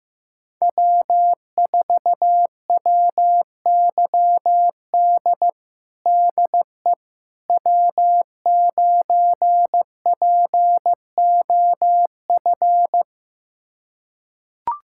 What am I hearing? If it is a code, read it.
Morse code, heard as a keyed tone, about 15 wpm, 700 Hz. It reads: W4WYD DE W9POF
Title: USA Callsign DE Callsign - 5 Characters - 15wpm